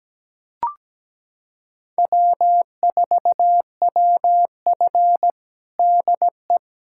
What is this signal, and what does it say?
Morse code, heard as a keyed tone, about 17 wpm, 700 Hz. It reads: W4WF DE
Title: USA Callsign DE Callsign - 4 Characters - 17wpm